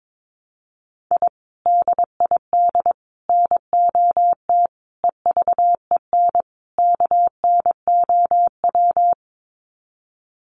Morse code, heard as a keyed tone, about 22 wpm, 700 Hz. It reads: I DID NOT EVEN KNOW
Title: Sets of 5 Words Encore 22wpm